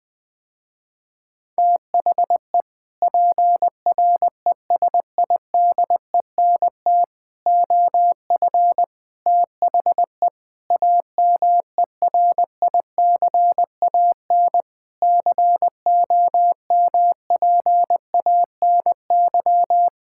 Morse code, heard as a keyed tone, about 20 wpm, 700 Hz. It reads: THE PRESIDENT OF THE AMERICAN COMPANY
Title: Sentences with 3-Word Phrases - Spoken Beginning - 20wpm